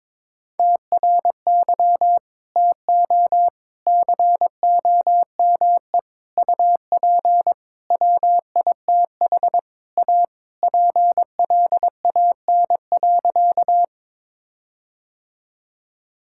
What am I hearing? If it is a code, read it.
Morse code, heard as a keyed tone, about 22 wpm, 700 Hz. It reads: TRY TO COME UP WITH A PLAN.
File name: practice / 22wpm / Sentences with 4-Word Phrases - 22wpm